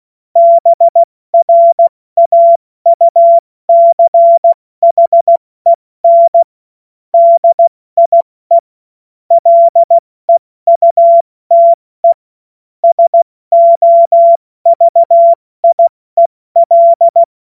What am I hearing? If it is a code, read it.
Morse code, heard as a keyed tone, about 16 wpm, 680 Hz. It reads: BRAUCHEN DIE LEUTE SOVIEL
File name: Morsekurs / Witze / Witz 01. Juni 2021, 70 F80